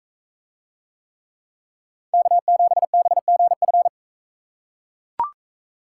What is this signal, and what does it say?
Morse code, heard as a keyed tone, about 41 wpm, 700 Hz. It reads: K7BGF